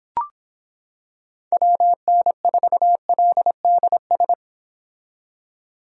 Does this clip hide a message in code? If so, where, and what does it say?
Yes: Morse code from 1.5 to 4.3 s, heard as a keyed tone, about 26 wpm, 700 Hz. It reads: WN4LDS